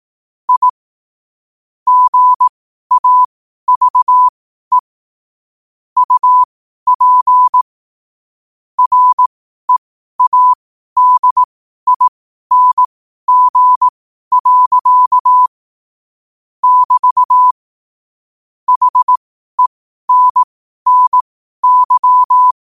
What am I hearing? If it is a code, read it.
Morse code, heard as a keyed tone, about 18 wpm, spaced as for 12 wpm, 1000 Hz. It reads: I GAVE UP READING. <BT> HENNY
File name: qotdfiles / 12wpm